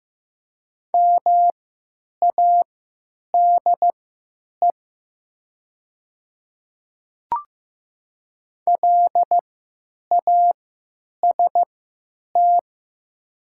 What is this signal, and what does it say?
Morse code, heard as a keyed tone, about 15 wpm, spaced as for 8 wpm, 700 Hz. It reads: MADE LAST
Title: 3x Character Spacing - Top 300 Words - 15wpm